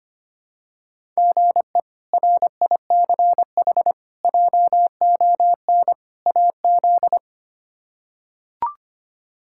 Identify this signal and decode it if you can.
Morse code, heard as a keyed tone, about 25 wpm, 700 Hz. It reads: GE RICH JON AZ
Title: K1USN SST QSO Practice - 25wpm